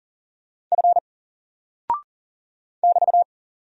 Morse code, heard as a keyed tone, about 39 wpm, 700 Hz. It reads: F <BT>
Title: Single Character (Letter:Number:Punctuation:Prosign) 40wpm